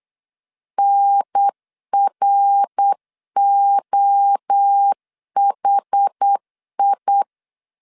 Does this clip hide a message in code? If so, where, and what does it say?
Yes: Morse code from 0.8 to 7.2 s, heard as a keyed tone, about 8 wpm, 800 Hz. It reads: NROHI